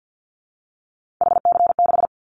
Morse code, heard as a keyed tone, about 50 wpm, 700 Hz. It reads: HC6
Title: Callsign Prefixes for Top 100 DXCC - 50wpm